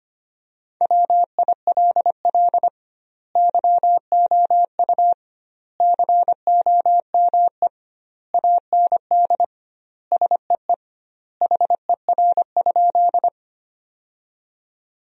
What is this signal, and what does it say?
Morse code, heard as a keyed tone, about 25 wpm, 700 Hz. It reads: WILL YOU COME AND SEE HER?
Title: 2x Word-Spacing - Sentences from Top 100 Words - 25wpm